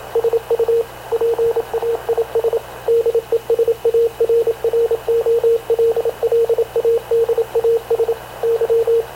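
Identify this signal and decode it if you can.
Morse code, heard as a keyed tone, about 26 wpm, 460 Hz, with radio noise. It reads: SU PAIS DESARROLLADAS Y
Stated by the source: Havana Radio / CLA sending Spanish press (PX) in Morse, recorded by Radio Officer Paul Dumesnil onboard CNAV QUEST/CZDO.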